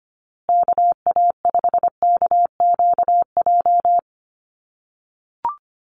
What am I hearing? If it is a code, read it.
Morse code, heard as a keyed tone, about 25 wpm, 700 Hz. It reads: KA5KQJ
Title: USA Call Signs Repeated 2x 25wpm